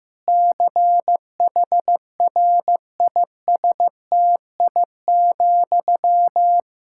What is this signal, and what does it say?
Morse code, heard as a keyed tone, about 15 wpm, 700 Hz. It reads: CHRISTI,
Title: Practice QSO - Exchange 1 - Beginning - 15wpm